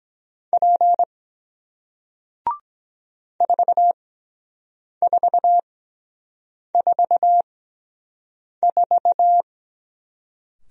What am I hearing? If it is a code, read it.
Morse code, heard as a keyed tone, about 23 wpm, spaced as for 7 wpm, 700 Hz. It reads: P 4444